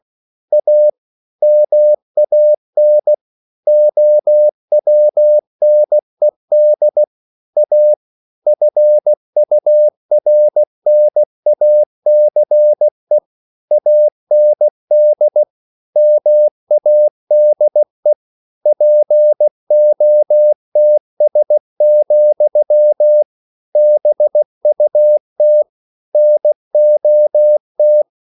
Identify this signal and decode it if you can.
Morse code, heard as a keyed tone, about 16 wpm, 600 Hz. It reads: A MAN OWNED A FURNACE AND MADE POTS, BUT NOT